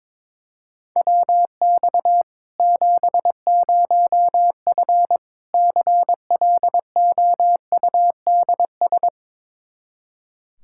Morse code, heard as a keyed tone, about 22 wpm, 700 Hz. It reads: WX 70F CLOUDS
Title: K1USN SST QSO Practice 22wpm